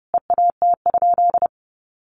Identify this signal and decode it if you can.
Morse code, heard as a keyed tone, about 30 wpm, 700 Hz. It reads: EAT?
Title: Sentences from Top 600 Words 30wpm